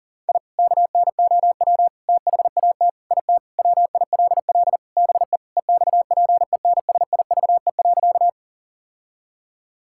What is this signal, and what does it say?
Morse code, heard as a keyed tone, about 40 wpm, 700 Hz. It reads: I KNOW THAT IT WILL BE EXPENSIVE.
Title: Sentences from Top 1000 Words - Review Part 6 - 40wpm